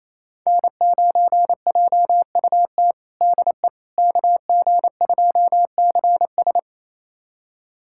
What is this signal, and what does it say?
Morse code, heard as a keyed tone, about 28 wpm, 700 Hz. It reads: N9JUT DE KG2CS
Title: USA Callsign DE Callsign - 5 Characters - 28wpm